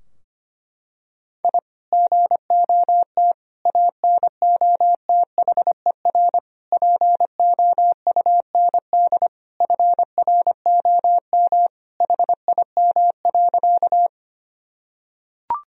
Morse code, heard as a keyed tone, about 25 wpm, 700 Hz. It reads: I GOT ANOTHER POUND FROM HIM.
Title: Sentences from Top 1000 Words - Review Part 2 - 25wpm